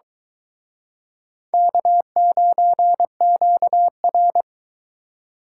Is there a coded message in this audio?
Yes: Morse code from 1.5 to 4.4 s, heard as a keyed tone, about 23 wpm, 700 Hz. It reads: K9QR